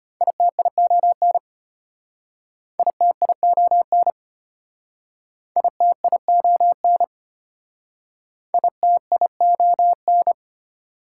Morse code, heard as a keyed tone, about 32 wpm, 700 Hz. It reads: ITION ITION ITION ITION